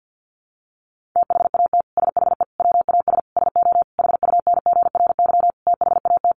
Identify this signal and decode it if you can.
Morse code, heard as a keyed tone, about 50 wpm, 700 Hz. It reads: THAT SHE WAS SO HUNGRY THAT
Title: Sentences from Top 1400 Words 50wpm